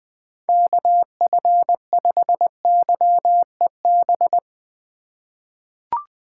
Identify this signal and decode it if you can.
Morse code, heard as a keyed tone, about 20 wpm, 700 Hz. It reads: KF5YEB
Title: USA Call Signs - 6 Characters - 2x Character Spacing - 20wpm